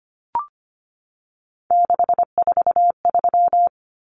Morse code, heard as a keyed tone, about 25 wpm, 700 Hz. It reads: 643
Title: Numbers - 3-Digit 25wpm